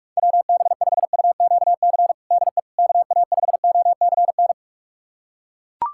Morse code, heard as a keyed tone, about 42 wpm, 700 Hz. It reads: WB5UQC DE KA5OCN
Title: USA Callsign DE Callsign - 6 Characters - 45wpm